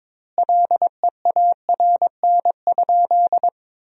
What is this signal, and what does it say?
Morse code, heard as a keyed tone, about 21 wpm, 700 Hz. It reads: LEARN?
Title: 2x Word-Spacing - Sentences from Top 300 Words - 22wpm